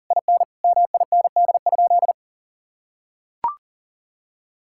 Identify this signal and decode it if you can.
Morse code, heard as a keyed tone, about 40 wpm, 700 Hz. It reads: IN MIND?
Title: Sentences from Top 1000 Words - Review Part 5 - 40wpm